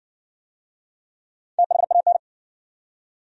Morse code, heard as a keyed tone, about 60 wpm, 700 Hz. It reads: THAN